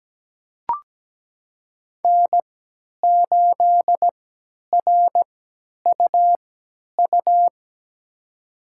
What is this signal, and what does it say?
Morse code, heard as a keyed tone, about 17 wpm, spaced as for 10 wpm, 700 Hz. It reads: N8RUU
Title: USA Call Signs - 5 Characters - 3x Character Spacing - 17wpm